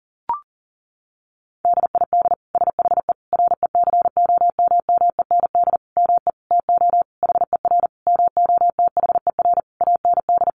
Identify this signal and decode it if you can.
Morse code, heard as a keyed tone, about 40 wpm, 700 Hz. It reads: DID SHE RECOMMEND ME TO HER MOTHER AND
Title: Sentences from Top 1200 Words 40wpm